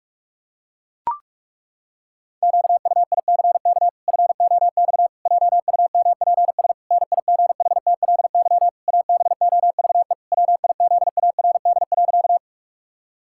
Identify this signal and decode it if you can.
Morse code, heard as a keyed tone, about 41 wpm, 700 Hz. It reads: QUICK FOX JUMPS NIGHTLY ABOVE WIZARD.
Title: Pangram Challenge v2 - 45wpm